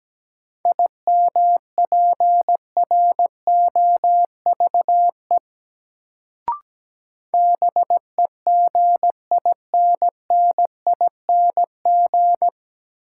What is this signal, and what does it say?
Morse code, heard as a keyed tone, about 17 wpm, 700 Hz. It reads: IMPROVE BEGINNING
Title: Sentences from Top 800 Words - New Word Drill - 17wpm